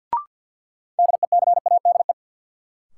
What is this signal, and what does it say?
Morse code, heard as a keyed tone, about 52 wpm, 700 Hz. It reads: DECADE
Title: Sets of 5 Words - New Word Drill - 50wpm